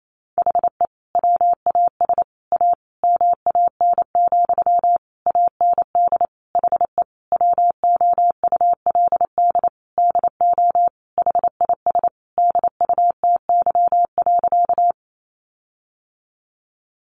Morse code, heard as a keyed tone, about 28 wpm, 700 Hz. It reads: HE WAS A MAN, AND HE WOULD DO HIS DUTY.
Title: Sentences from Top 1500 Words 28wpm